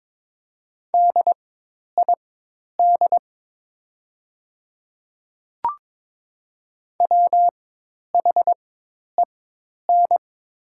Morse code, heard as a keyed tone, about 22 wpm, spaced as for 10 wpm, 700 Hz. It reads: DID WHEN
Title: 4x Character Spacing - Top 100 Words - 22wpm